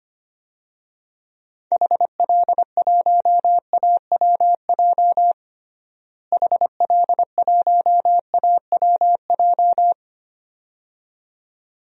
Morse code, heard as a keyed tone, about 25 wpm, 700 Hz. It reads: HL1AWJ HL1AWJ
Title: DXCC-100 Call Signs - 6 Characters - Repeated 2x - 25wpm